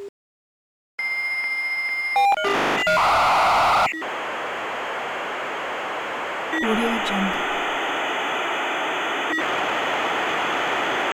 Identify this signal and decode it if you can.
Dial-up modem handshake, with speech.
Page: modem-connection-dial-up-and-connect